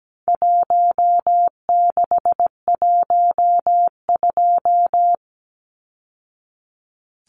Morse code, heard as a keyed tone, about 17 wpm, 700 Hz. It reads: 1612
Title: Numbers - Contest Style - 4-Digit 17wpm